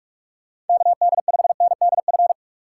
Morse code, heard as a keyed tone, about 46 wpm, 700 Hz. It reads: KD5NDF
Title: USA Call Signs - 6 Characters - 4x Character Spacing - 45wpm